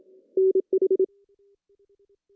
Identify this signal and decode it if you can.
Morse code, heard as a keyed tone, about 26 wpm, 380 Hz. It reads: NH
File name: CW / cqww-cw-k5tr